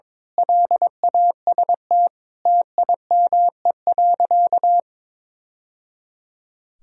Morse code, heard as a keyed tone, about 22 wpm, 700 Hz. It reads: LAST TIME.
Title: Sentences from Top 1000 Words - Review Part 2 - 22wpm